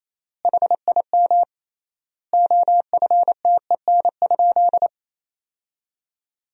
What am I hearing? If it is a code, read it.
Morse code, heard as a keyed tone, about 28 wpm, 700 Hz. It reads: HIM OFTEN?
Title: 3x Word-Spacing - Sentences from Top 400 Words - 28wpm